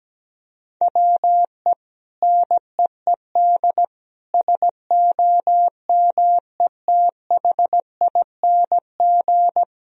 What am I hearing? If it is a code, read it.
Morse code, heard as a keyed tone, about 17 wpm, 700 Hz. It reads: WE NEED SOMETHING
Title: Sentences from Top 1000 Words - Review Part 4 - 17wpm